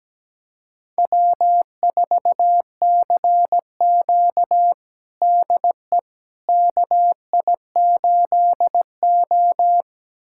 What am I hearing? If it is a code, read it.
Morse code, heard as a keyed tone, about 17 wpm, 700 Hz. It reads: W4CQ DE KI8O
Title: USA Callsign DE Callsign - 4 Characters - 17wpm